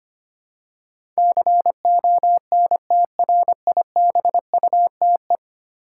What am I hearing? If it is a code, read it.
Morse code, heard as a keyed tone, about 25 wpm, 700 Hz. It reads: CONTRIBUTE
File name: 10 Letter Words - First 3 Letters Given - 25wpm